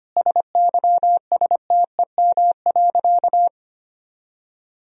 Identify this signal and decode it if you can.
Morse code, heard as a keyed tone, about 25 wpm, 700 Hz. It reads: SYSTEM.
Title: Sentences from Top 1000 Words - Review Part 4 - 25wpm